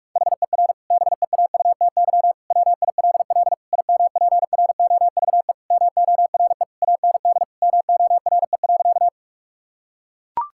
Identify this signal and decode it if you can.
Morse code, heard as a keyed tone, about 42 wpm, 700 Hz. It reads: HER BEAUTY WILL IMPROVE MORE AND MORE.
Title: Sentences from Top 1000 Words - Review Part 5 - 45wpm